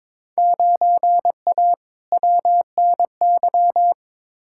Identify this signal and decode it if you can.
Morse code, heard as a keyed tone, about 22 wpm, 700 Hz. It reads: 9A WNY
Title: Field Day Exchanges Encore - 22wpm